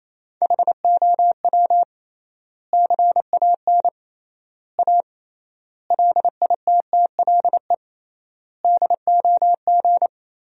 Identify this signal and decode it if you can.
Morse code, heard as a keyed tone, about 28 wpm, 700 Hz. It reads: HOW CAN A LITTLE DOG
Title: 3x Word-Spacing - Sentences from Top 500 Words - 28wpm